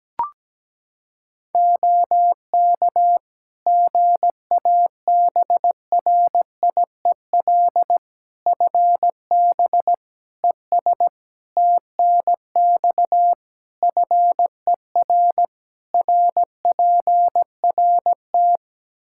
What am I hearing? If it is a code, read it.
Morse code, heard as a keyed tone, about 17 wpm, 700 Hz. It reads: OK GABRIEL FB ES TNX FER RPRT
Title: Practice QSO - Exchange 2 - Info - 17wpm